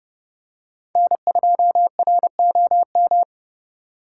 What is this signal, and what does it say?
Morse code, heard as a keyed tone, about 31 wpm, 700 Hz. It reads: N2ROM